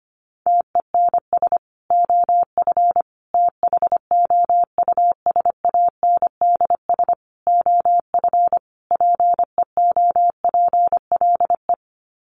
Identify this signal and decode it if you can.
Morse code, heard as a keyed tone, about 25 wpm, 700 Hz. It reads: TENS OF THOUSANDS OF PEOPLE
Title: Sets of 5 Words Encore2 25wpm